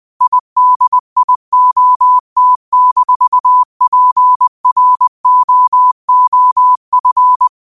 Morse code, heard as a keyed tone, about 20 wpm, 1000 Hz. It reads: IDIOT-PROOF
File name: qotdfiles / 20wpm